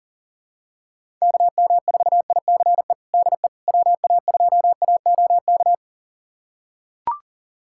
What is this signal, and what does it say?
Morse code, heard as a keyed tone, about 40 wpm, 700 Hz. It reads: KM4ICE DE WA2AOK
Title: USA Callsign DE Callsign - 6 Characters - 40wpm